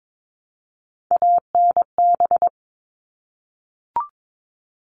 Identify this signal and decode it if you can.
Morse code, heard as a keyed tone, about 22 wpm, 700 Hz. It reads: ANB
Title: All Cut Numbers - 3-Digit 22wpm